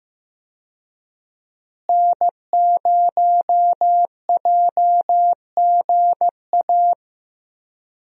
Morse code, heard as a keyed tone, about 15 wpm, 700 Hz. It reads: N0JGA
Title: USA Call Signs - 5 Characters - Repeated 3x - 15wpm